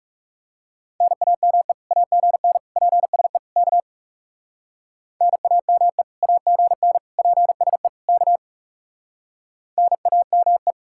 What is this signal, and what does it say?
Morse code, heard as a keyed tone, about 38 wpm, 700 Hz. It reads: NAME AGN PSE K NAME AGN PSE K NAME